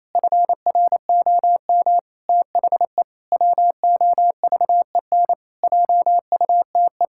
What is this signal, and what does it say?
Morse code, heard as a keyed tone, about 28 wpm, 700 Hz. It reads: FROM THE WOVEN JUTE
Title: Pangram Challenge v2 - 28wpm